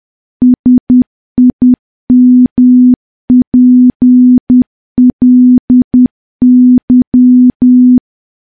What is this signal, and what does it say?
Morse code, heard as a keyed tone, about 10 wpm, 250 Hz. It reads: SIMPLY
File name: morse / morse-training-10WPM